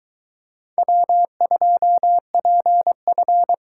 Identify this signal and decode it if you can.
Morse code, heard as a keyed tone, about 24 wpm, 700 Hz. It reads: W2PF